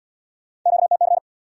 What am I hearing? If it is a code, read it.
Morse code, heard as a keyed tone, about 62 wpm, 700 Hz. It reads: QTC
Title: Speed-Racing Q-Codes - 50wpm